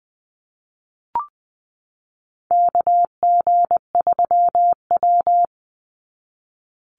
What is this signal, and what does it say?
Morse code, heard as a keyed tone, about 20 wpm, 700 Hz. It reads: KG3W
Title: NA SOTA Chaser Callsigns 20wpm